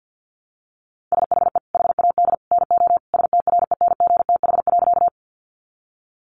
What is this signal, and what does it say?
Morse code, heard as a keyed tone, about 50 wpm, 700 Hz. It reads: SHE HAD NO STRENGTH.
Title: Sentences from Top 1300 Words 50wpm